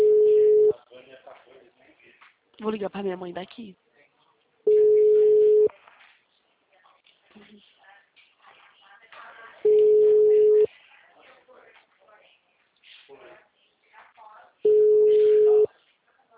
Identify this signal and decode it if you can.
Ringback tone (Europe, 425 Hz), with speech.
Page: Limeira 4 - SP